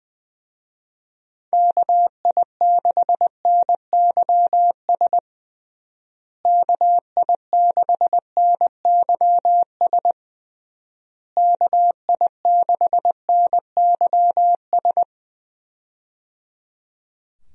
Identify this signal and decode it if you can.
Morse code, heard as a keyed tone, about 20 wpm, 700 Hz. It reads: KI6NYS KI6NYS KI6NYS
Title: USA Call Signs - 6 Characters - Repeated 3x - 20wpm